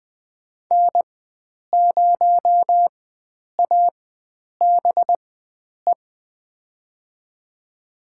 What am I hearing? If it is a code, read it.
Morse code, heard as a keyed tone, about 21 wpm, spaced as for 9 wpm, 700 Hz. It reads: N0ABE
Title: USA Call Signs - 4x Character Spacing - 20wpm